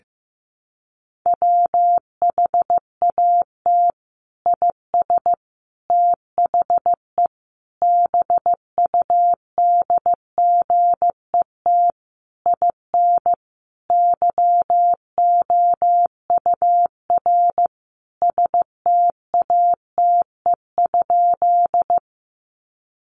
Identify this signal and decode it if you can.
Morse code, heard as a keyed tone, about 15 wpm, 700 Hz. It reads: WHAT IS THE BUDGET IN YOUR STATE?
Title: Sentences from Top 1200 Words 15wpm